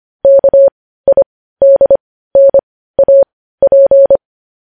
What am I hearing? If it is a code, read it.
Morse code, heard as a keyed tone, about 25 wpm, spaced as for 15 wpm, 550 Hz. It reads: KIDNAP
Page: news-headlines-in-morse-code-at-15-wpm